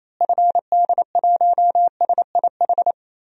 Morse code, heard as a keyed tone, about 27 wpm, 700 Hz. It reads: FD1SIH
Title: DXCC-100 Call Signs - 6 Characters - Repeated 3x - 28wpm